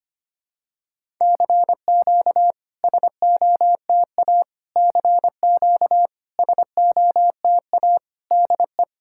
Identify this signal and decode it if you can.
Morse code, heard as a keyed tone, about 25 wpm, 690 Hz. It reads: CQ SOTA CQ SOTA DE
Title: SOTA QSO Practice 25wpm